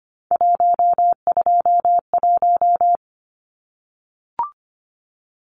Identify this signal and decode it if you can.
Morse code, heard as a keyed tone, about 25 wpm, 700 Hz. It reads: 121
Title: Numbers - 3-Digit 25wpm